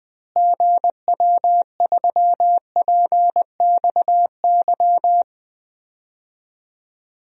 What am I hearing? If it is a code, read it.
Morse code, heard as a keyed tone, about 20 wpm, 700 Hz. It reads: GW3PXY
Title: DXCC-100 Call Signs - 6 Characters - Repeated 3x - 20wpm